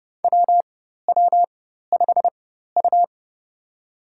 Morse code, heard as a keyed tone, about 30 wpm, spaced as for 14 wpm, 700 Hz. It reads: WW5U
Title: USA Call Signs - 4 Characters - 4x Character Spacing - 30wpm